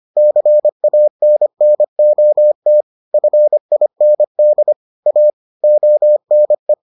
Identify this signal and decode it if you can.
Morse code, heard as a keyed tone, about 25 wpm, 600 Hz. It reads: CANNOT FIND A ONE